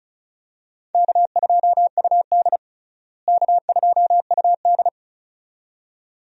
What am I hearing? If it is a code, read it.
Morse code, heard as a keyed tone, about 36 wpm, 700 Hz. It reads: K2UD K2UD
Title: USA Call Signs - 4 Characters - Repeated 2x - 35wpm